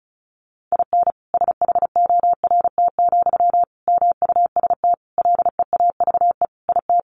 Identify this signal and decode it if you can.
Morse code, heard as a keyed tone, about 34 wpm, 700 Hz. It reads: IN SHORT, MUST LEAVE IT
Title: Sentences from Top 1000 Words 35wpm